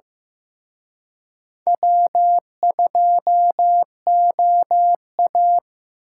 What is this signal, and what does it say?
Morse code, heard as a keyed tone, about 18 wpm, 700 Hz. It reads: W2OA
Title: Speed Racing - USA 1x2 Call Signs 15wpm